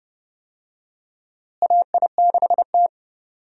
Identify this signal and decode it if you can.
Morse code, heard as a keyed tone, about 32 wpm, 700 Hz. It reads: AI6T
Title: USA Call Signs - 4 Characters - Repeated 3x - 30wpm